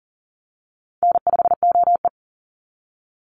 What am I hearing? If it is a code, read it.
Morse code, heard as a keyed tone, about 40 wpm, 700 Hz. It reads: N5OE
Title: Easy Call Signs Repeated 2x - 40wpm